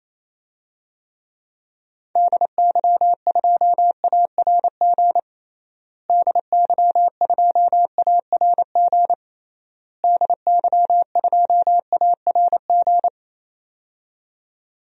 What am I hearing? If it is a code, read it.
Morse code, heard as a keyed tone, about 27 wpm, 700 Hz. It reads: DY2ARG DY2ARG DY2ARG
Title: DXCC-100 Call Signs - 6 Characters - Repeated 3x - 28wpm